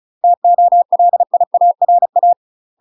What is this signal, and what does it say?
Morse code, heard as a keyed tone, about 36 wpm, 700 Hz. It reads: TOLIARA
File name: UK MASTER QSOs TRAINING / 35WPM / UK MASTER QSOs 0029